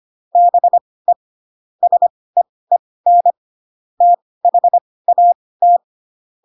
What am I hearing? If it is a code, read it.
Morse code, heard as a keyed tone, about 25 wpm, spaced as for 18 wpm, 700 Hz. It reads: BE SEEN THAT
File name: Pinocchio / Pinocchio ch1 12 25 ik1hge